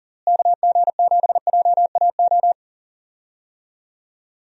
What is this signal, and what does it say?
Morse code, heard as a keyed tone, about 40 wpm, 700 Hz. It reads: KG7JAO
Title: USA Call Signs - 6 Characters - Repeated 3x - 40wpm